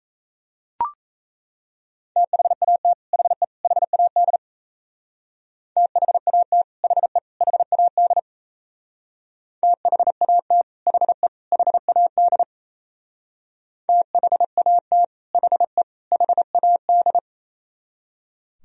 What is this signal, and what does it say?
Morse code, heard as a keyed tone, about 37 wpm, 700 Hz. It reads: THAT HE HAD THAT HE HAD THAT HE HAD THAT HE HAD